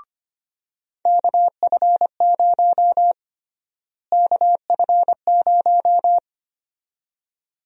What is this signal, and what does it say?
Morse code, heard as a keyed tone, about 25 wpm, 700 Hz. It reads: KF0 KF0
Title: USA Call Sign Prefixes - Repeated 2x - 25wpm